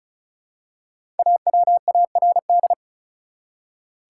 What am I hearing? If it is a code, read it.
Morse code, heard as a keyed tone, about 36 wpm, 700 Hz. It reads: AWARD
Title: Five Letter Words - 35wpm